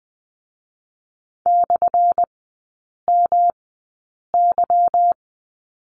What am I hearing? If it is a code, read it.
Morse code, heard as a keyed tone, about 20 wpm, spaced as for 8 wpm, 700 Hz. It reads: /MY
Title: 178 Lesson 29e - Slash - Three Characters 20wpm